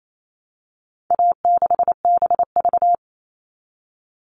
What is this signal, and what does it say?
Morse code, heard as a keyed tone, about 28 wpm, 700 Hz. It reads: A6BV
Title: All Cut Numbers - 4-Digit 28wpm